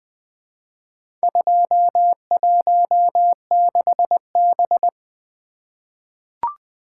Morse code, heard as a keyed tone, about 21 wpm, 700 Hz. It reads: 216B
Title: Four Letter-Number Combinations - 20wpm